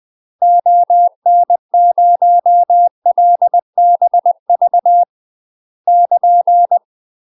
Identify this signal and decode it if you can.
Morse code, heard as a keyed tone, about 20 wpm, 700 Hz. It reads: ON0LBV <KN>
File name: UK MASTER QSOs TRAINING/20WPM